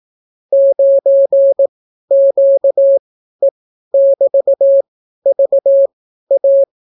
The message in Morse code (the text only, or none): 9QE<BT>VA